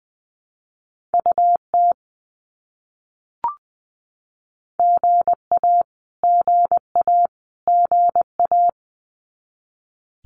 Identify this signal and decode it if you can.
Morse code, heard as a keyed tone, about 20 wpm, 700 Hz. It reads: UT GA GA GA
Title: US States Repeated 3x 20wpm